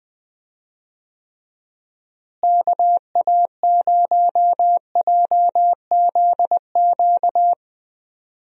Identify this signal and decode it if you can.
Morse code, heard as a keyed tone, about 20 wpm, 700 Hz. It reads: KA0JZQ
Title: USA Call Signs - 6 Characters - Repeated 3x - 20wpm